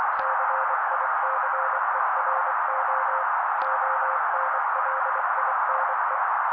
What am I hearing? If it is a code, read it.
Morse code, heard as a keyed tone, about 24 wpm, 530 Hz. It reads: CICERO ONLINE
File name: daily / 23wpm noise